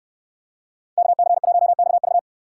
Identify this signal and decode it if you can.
Morse code, heard as a keyed tone, about 69 wpm, 700 Hz. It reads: KC1XV